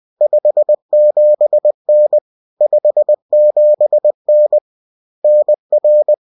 Morse code, heard as a keyed tone, about 20 wpm, 600 Hz. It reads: 57N 57N NR